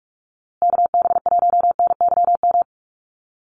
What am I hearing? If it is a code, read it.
Morse code, heard as a keyed tone, about 44 wpm, 700 Hz. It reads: KB1NYM
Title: USA Call Signs Repeated 2x 45wpm